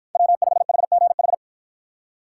Morse code, heard as a keyed tone, about 52 wpm, 700 Hz. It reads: W5HGH